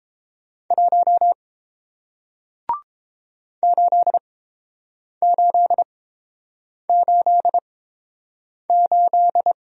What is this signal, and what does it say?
Morse code, heard as a keyed tone, about 29 wpm, spaced as for 7 wpm, 700 Hz. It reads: 1 8888